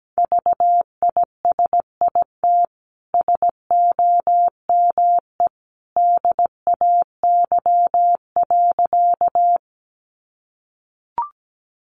Morse code, heard as a keyed tone, about 17 wpm, 700 Hz. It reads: VISIT SOME DAY.
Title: Sentences from Top 900 Words 17wpm